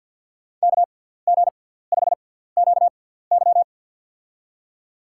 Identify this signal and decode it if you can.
Morse code, heard as a keyed tone, about 52 wpm, spaced as for 17 wpm, 700 Hz. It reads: KG5QY
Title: USA Call Signs - 6x Character Spacing - 50wpm